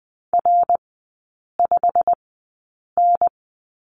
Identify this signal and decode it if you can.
Morse code, heard as a keyed tone, about 20 wpm, spaced as for 8 wpm, 700 Hz. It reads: R5N
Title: 087 Lesson 15e - Number 5 - Three Characters 20wpm